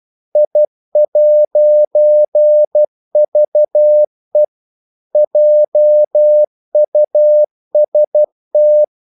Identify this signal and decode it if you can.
Morse code, heard as a keyed tone, about 12 wpm, 600 Hz. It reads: I'VE JUST
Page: English quotations in CW (telegraphy - morse)